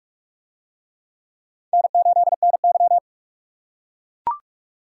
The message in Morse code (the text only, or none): N8NY